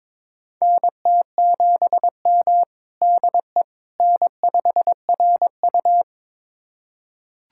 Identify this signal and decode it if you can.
Morse code, heard as a keyed tone, about 22 wpm, 700 Hz. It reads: NT7M DE N5RU
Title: USA Callsign DE Callsign - 4 Characters - 22wpm